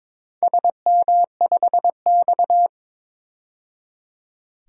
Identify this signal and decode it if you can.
Morse code, heard as a keyed tone, about 22 wpm, 700 Hz. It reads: SM5X
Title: DXCC-100 Call Signs - 4 Characters - 22wpm